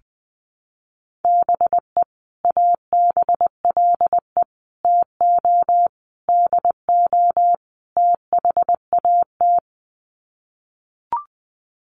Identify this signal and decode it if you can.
Morse code, heard as a keyed tone, about 20 wpm, 700 Hz. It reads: BE ABLE TO DO THAT
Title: Sets of 5 Words Encore2 20wpm